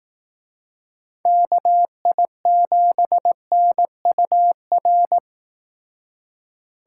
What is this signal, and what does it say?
Morse code, heard as a keyed tone, about 18 wpm, 700 Hz. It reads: KI7NUR
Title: Speed Racing - USA 2x3 Call Signs 15wpm